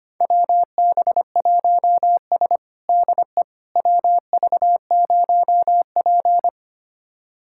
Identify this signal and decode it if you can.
Morse code, heard as a keyed tone, about 25 wpm, 700 Hz. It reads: WB1S DE WV0P
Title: USA Callsign DE Callsign - 4 Characters - 25wpm